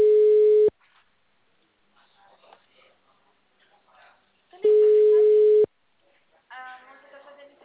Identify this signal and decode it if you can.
Ringback tone (Europe, 425 Hz), with speech.
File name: out- -2014- - - .25521